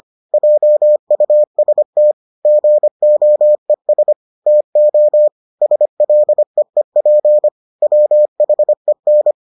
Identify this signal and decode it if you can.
Morse code, heard as a keyed tone, about 25 wpm, 600 Hz. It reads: JUST GOES TO SLEEP WHEN